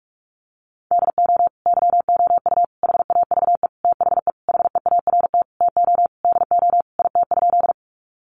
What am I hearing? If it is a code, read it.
Morse code, heard as a keyed tone, about 44 wpm, 700 Hz. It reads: DO YOU HAVE THE HEART TO DO IT?
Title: Sentences from Top 1200 Words 45wpm